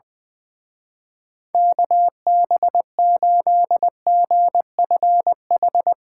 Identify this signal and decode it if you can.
Morse code, heard as a keyed tone, about 20 wpm, 700 Hz. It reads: KB8GFH
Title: Speed Racing - USA 2x3 Call Signs 20wpm